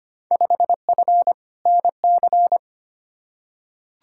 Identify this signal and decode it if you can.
Morse code, heard as a keyed tone, about 25 wpm, 700 Hz. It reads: 5F NC
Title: Field Day Exchanges Encore - 25wpm